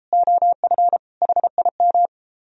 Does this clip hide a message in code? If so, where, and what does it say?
Yes: Morse code from 0.1 to 2.1 s, heard as a keyed tone, about 33 wpm, 700 Hz. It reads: OF HIM